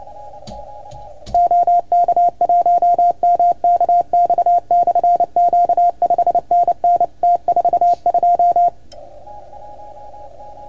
Morse code, heard as a keyed tone, about 29 wpm, 690 Hz. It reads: OK1MKX/Q 5NN T42
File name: Archiv / CUC310 / OK1FGD CUC 310 pileup